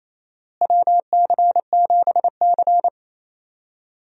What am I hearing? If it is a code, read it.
Morse code, heard as a keyed tone, about 28 wpm, 700 Hz. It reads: WC7C
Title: USA Call Signs - 4 Characters - Repeated 3x - 28wpm